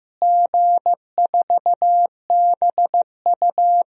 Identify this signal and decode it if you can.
Morse code, heard as a keyed tone, about 15 wpm, 700 Hz. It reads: G4BU
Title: DXCC-100 Call Signs - 4 Characters - Repeated 3x - 15wpm